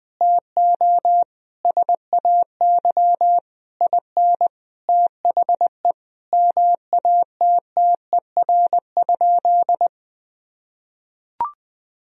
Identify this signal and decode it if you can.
Morse code, heard as a keyed tone, about 20 wpm, 700 Hz. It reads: TO SAY IN THE MATTER?
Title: Sentences from Top 1000 Words - Review Part 3 - 20wpm